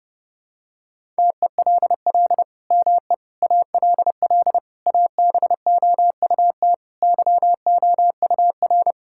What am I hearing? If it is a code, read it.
Morse code, heard as a keyed tone, about 30 wpm, 700 Hz. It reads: TELL ME ALL ABOUT YOUR
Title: Sentences from Top 1000 Words - Review Part 5 - 30wpm